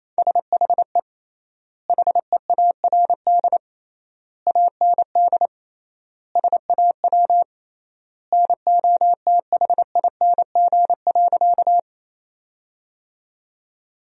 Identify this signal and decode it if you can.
Morse code, heard as a keyed tone, about 28 wpm, 700 Hz. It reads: SHE HEARD AND SAW NOTHING.
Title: 3x Word-Spacing - Sentences from Top 500 Words - 28wpm